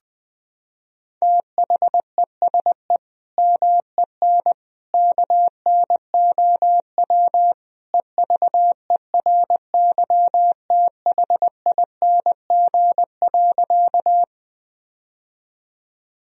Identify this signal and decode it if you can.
Morse code, heard as a keyed tone, about 20 wpm, 700 Hz. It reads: THESE MEN KNOW EVERYTHING.
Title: Sentences from Top 1000 Words - Review Part 1 - 20wpm